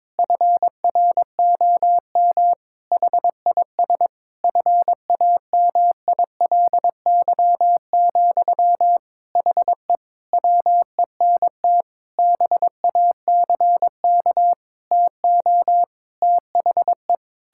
Morse code, heard as a keyed tone, about 22 wpm, 700 Hz. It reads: FROM HIS FAMILY, HE WENT BACK TO THE
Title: Sentences from Top 1000 Words - Review Part 6 - 22wpm